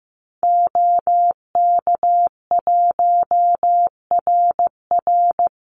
Morse code, heard as a keyed tone, about 15 wpm, 700 Hz. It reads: OK1RR
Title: Call Signs Repeated 2x - 15wpm